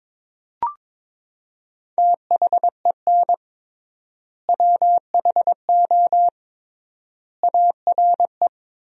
Morse code, heard as a keyed tone, about 21 wpm, 700 Hz. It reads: THEN WHO ARE
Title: 3x Word-Spacing - Sentences from Top 100 Words - 22wpm